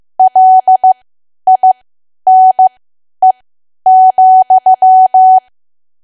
Morse code, heard as a keyed tone, about 15 wpm, spaced as for 10 wpm, 750 Hz. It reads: LINE,
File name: Archive / 10 WPM / 10WPM